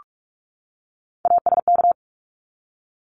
Morse code, heard as a keyed tone, about 44 wpm, 700 Hz. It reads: ASK
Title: ED Suffix - Top 100 Words with Base Words 45wpm